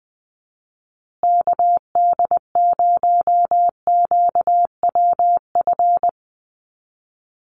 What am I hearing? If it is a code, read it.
Morse code, heard as a keyed tone, about 20 wpm, 700 Hz. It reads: KD0QWF
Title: USA Call Signs Repeated 3x 20wpm